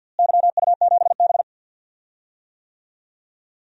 Morse code, heard as a keyed tone, about 52 wpm, 700 Hz. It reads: YU7B
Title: DXCC-100 Call Signs - 4 Characters - Repeated 3x - 50wpm